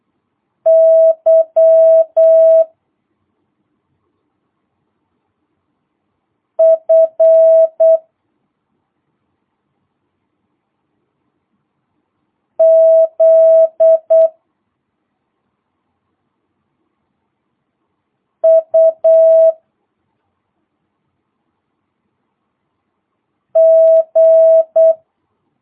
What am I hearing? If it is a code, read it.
Morse code, heard as a keyed tone, about 8 wpm, 650 Hz. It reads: Y F Z U G